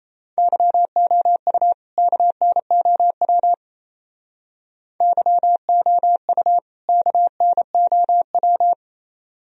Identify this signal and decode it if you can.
Morse code, heard as a keyed tone, about 34 wpm, 700 Hz. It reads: YOU KNOW YOU KNOW
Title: Speed-Racing - Sets of 2 Words 28wpm